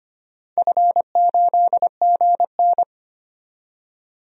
Morse code, heard as a keyed tone, about 25 wpm, 700 Hz. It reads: F8GN
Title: DXCC-100 Call Signs - 4 Characters - 25wpm